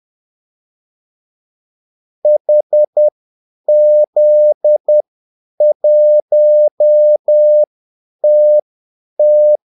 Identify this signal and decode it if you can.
Morse code, heard as a keyed tone, about 10 wpm, spaced as for 8 wpm, 600 Hz. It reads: HZ1TT
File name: Callsigns-10-8wpm